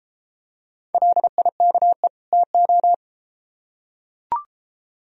Morse code, heard as a keyed tone, about 33 wpm, 700 Hz. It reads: LIKE TO